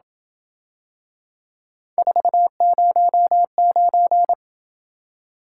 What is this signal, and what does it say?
Morse code, heard as a keyed tone, about 27 wpm, 700 Hz. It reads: 409